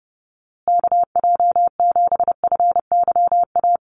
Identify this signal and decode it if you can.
Morse code, heard as a keyed tone, about 29 wpm, 700 Hz. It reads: KJ7FYA
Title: USA Call Signs Repeated 2x 30wpm